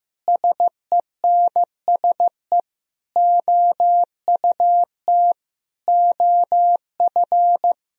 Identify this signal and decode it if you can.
Morse code, heard as a keyed tone, about 15 wpm, 700 Hz. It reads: SENSE OUT OF
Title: Sentences from Top 1000 Words - Review Part 4 - 15wpm